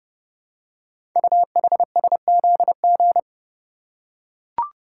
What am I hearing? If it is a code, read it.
Morse code, heard as a keyed tone, about 28 wpm, 700 Hz. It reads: UHSZG
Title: Code Groups - 5 Letters - 30wpm